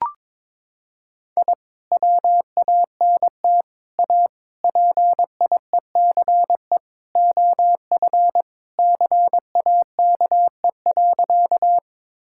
Morse code, heard as a keyed tone, about 22 wpm, 700 Hz. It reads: I WANT A PIECE OF CAKE.
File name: practice / 22wpm / Sentences with 3-Word Phrases - 22wpm